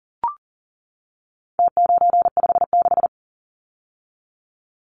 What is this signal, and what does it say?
Morse code, heard as a keyed tone, about 40 wpm, 700 Hz. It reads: T956
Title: Numbers - Contest Style - 4-Digit 40wpm